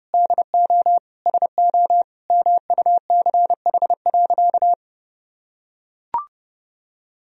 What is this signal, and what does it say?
Morse code, heard as a keyed tone, about 30 wpm, 700 Hz. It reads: DO SO MUCH.
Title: Sentences from Top 1000 Words - Review Part 3 - 30wpm